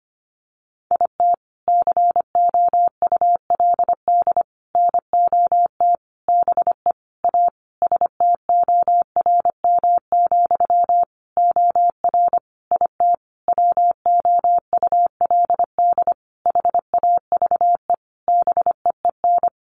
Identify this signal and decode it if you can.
Morse code, heard as a keyed tone, about 25 wpm, 700 Hz. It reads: IT COULD NOT BE A STORM, OR IT WOULD HAVE BEEN
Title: Sentences from Top 1500 Words 25wpm